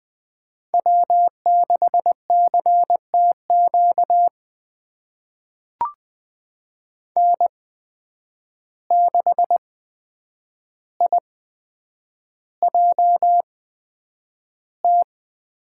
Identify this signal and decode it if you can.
Morse code, heard as a keyed tone, about 20 wpm, spaced as for 9 wpm, 700 Hz. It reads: W6CTQ N6IJT
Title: USA Call Signs - 5 Characters - 8x Character Spacing - 20wpm